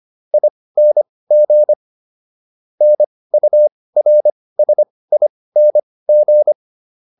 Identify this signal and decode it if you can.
Morse code, heard as a keyed tone, about 25 wpm, spaced as for 18 wpm, 600 Hz. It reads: ING NURSING